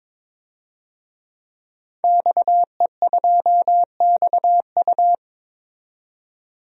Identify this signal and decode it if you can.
Morse code, heard as a keyed tone, about 22 wpm, 700 Hz. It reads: XE2XU
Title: DXCC-100 Call Signs - 5 Characters - 22wpm